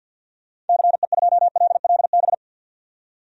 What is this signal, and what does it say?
Morse code, heard as a keyed tone, about 53 wpm, 700 Hz. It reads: CE2LLB
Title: DXCC-100 Call Signs - 6 Characters - 50wpm